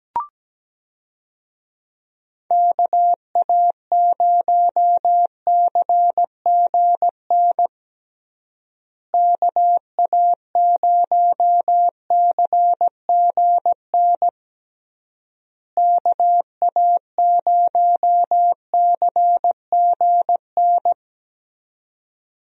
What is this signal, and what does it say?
Morse code, heard as a keyed tone, about 17 wpm, 700 Hz. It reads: KA0CGN KA0CGN KA0CGN
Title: USA Call Signs - 6 Characters - Repeated 3x - 17wpm